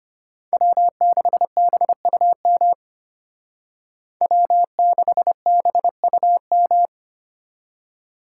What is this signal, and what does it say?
Morse code, heard as a keyed tone, about 32 wpm, 700 Hz. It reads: W6BUM W6BUM